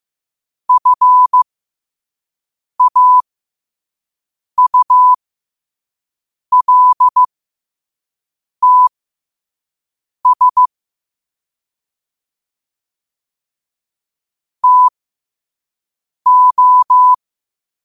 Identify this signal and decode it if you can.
Morse code, heard as a keyed tone, about 15 wpm, spaced as for 5 wpm, 1000 Hz. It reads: FAULTS TO
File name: qotdfiles / 5wpm